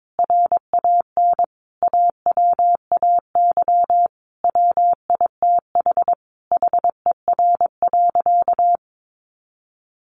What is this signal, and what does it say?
Morse code, heard as a keyed tone, about 22 wpm, 700 Hz. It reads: RAN AWAY WITH HER.
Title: Sentences from Top 1500 Words 22wpm